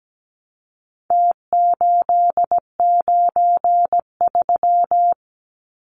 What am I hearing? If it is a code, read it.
Morse code, heard as a keyed tone, about 17 wpm, 700 Hz. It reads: T893
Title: Numbers - Contest Style - 4-Digit 17wpm